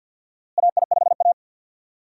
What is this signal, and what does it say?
Morse code, heard as a keyed tone, about 47 wpm, 700 Hz. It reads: AI5A
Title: USA Call Signs - 4 Characters - Repeated 2x - 50wpm